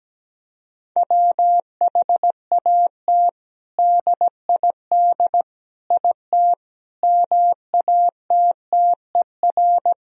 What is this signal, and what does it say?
Morse code, heard as a keyed tone, about 17 wpm, 700 Hz. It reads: WHAT DID IT MATTER
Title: Sentences from Top 1000 Words - Review Part 3 - 17wpm